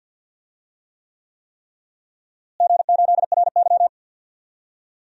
Morse code, heard as a keyed tone, about 52 wpm, 700 Hz. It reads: G8RY
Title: DXCC-100 Call Signs - 4 Characters - Repeated 3x - 50wpm